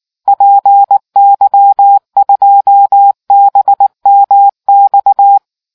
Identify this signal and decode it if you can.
Morse code, heard as a keyed tone, about 19 wpm, 800 Hz. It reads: PY2BMX